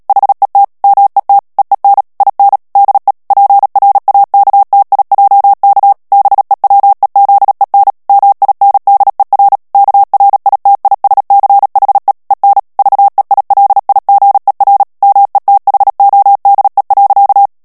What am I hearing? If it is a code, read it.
Morse code, heard as a keyed tone, about 37 wpm, 800 Hz. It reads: HET MET EEN IN DE PRAKTIJK BEWEZEN MINDER KRITISCHE EN VEILIGER METHODE.
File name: morse mp3 / HOLLANDAIS texte clair 2 vitesse 35 mots minute